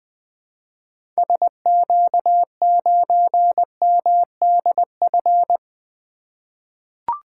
Morse code, heard as a keyed tone, about 20 wpm, 700 Hz. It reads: SQ9MDF
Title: 2023 SOTA Activator Call Signs - 20wpm